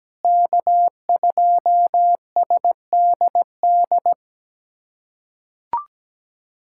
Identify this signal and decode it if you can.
Morse code, heard as a keyed tone, about 17 wpm, 700 Hz. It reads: K2SDD
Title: USA Callsign DE Callsign - 5 Characters - 17wpm